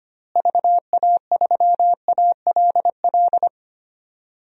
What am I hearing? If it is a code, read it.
Morse code, heard as a keyed tone, about 25 wpm, 700 Hz. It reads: VA3ALL
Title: DXCC-100 Call Signs - 6 Characters - Repeated 2x - 25wpm